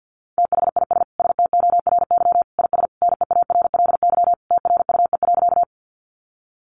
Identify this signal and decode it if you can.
Morse code, heard as a keyed tone, about 50 wpm, 700 Hz. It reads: THIS STORY IS NEARLY TRUE.
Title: Sentences from Top 1300 Words 50wpm